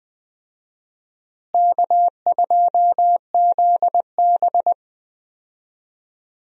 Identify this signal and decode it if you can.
Morse code, heard as a keyed tone, about 21 wpm, 700 Hz. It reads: K2ZB
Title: USA Call Signs - 4 Characters - Repeated 3x - 20wpm